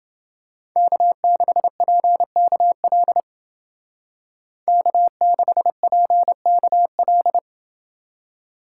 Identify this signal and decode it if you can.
Morse code, heard as a keyed tone, about 28 wpm, 700 Hz. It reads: K6PKL K6PKL